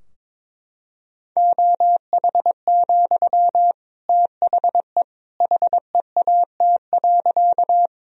Morse code, heard as a keyed tone, about 22 wpm, 700 Hz. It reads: OH, THE HEAT.
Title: Sentences from Top 1000 Words - Review Part 3 - 22wpm